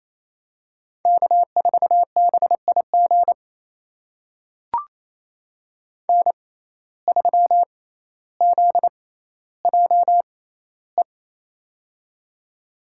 Morse code, heard as a keyed tone, about 27 wpm, spaced as for 14 wpm, 700 Hz. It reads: K4BIG N3ZJE
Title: USA Call Signs - 5 Characters - 6x Character Spacing - 28wpm